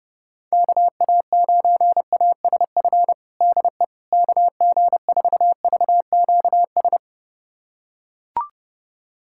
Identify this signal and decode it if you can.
Morse code, heard as a keyed tone, about 30 wpm, 700 Hz. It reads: KA9ASF DE KG4VQS
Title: USA Callsign DE Callsign 30wpm